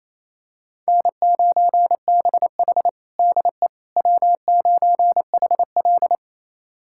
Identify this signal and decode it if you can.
Morse code, heard as a keyed tone, about 28 wpm, 700 Hz. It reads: N9BH DE W9HL
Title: USA Callsign DE Callsign - 4 Characters - 28wpm